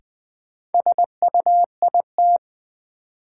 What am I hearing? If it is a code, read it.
Morse code, heard as a keyed tone, about 20 wpm, 700 Hz. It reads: SUIT
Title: Sentences from Top 1600 Words - New Word Drill - 20wpm